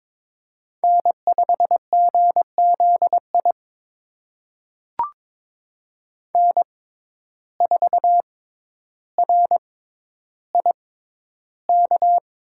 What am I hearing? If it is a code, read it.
Morse code, heard as a keyed tone, about 22 wpm, spaced as for 11 wpm, 700 Hz. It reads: N5GZI N4RIK
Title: USA Call Signs - 5 Characters - 6x Character Spacing - 22wpm